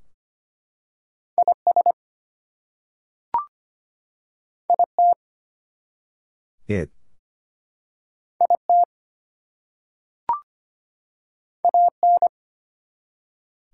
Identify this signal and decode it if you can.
Morse code, heard as a keyed tone, about 25 wpm, 700 Hz. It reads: IS IT IT AN
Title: Two Letter Words - 25wpm